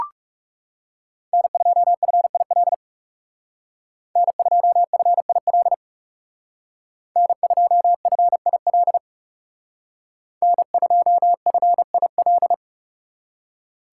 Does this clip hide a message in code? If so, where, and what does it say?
Yes: Morse code from 1.3 to 12.5 s, heard as a keyed tone, about 38 wpm, 700 Hz. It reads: N2FIL N2FIL N2FIL N2FIL